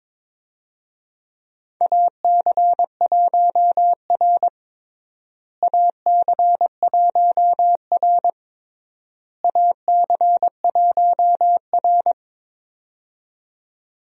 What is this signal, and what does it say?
Morse code, heard as a keyed tone, about 21 wpm, 700 Hz. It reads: AC1R AC1R AC1R
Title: USA Call Signs - 4 Characters - Repeated 3x - 22wpm